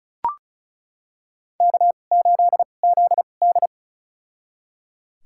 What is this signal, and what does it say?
Morse code, heard as a keyed tone, about 36 wpm, spaced as for 25 wpm, 700 Hz. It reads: K8ZD
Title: USA Call Signs - 4 Characters - 2x Character Spacing - 35wpm